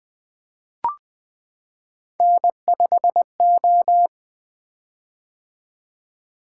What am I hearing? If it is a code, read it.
Morse code, heard as a keyed tone, about 20 wpm, 700 Hz. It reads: N5O
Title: Three Letter-Number Combinations - 20wpm